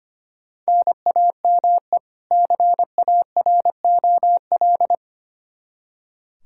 Morse code, heard as a keyed tone, about 25 wpm, 700 Hz. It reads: NAME CAROL
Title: Practice QSO - Exchange 1 - Beginning - 25wpm